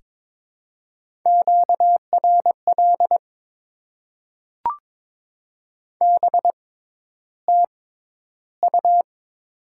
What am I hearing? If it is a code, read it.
Morse code, heard as a keyed tone, about 22 wpm, spaced as for 11 wpm, 700 Hz. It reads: QRL BTU
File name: Top 20 QSO Elements - 6x Character Spacing - 22wpm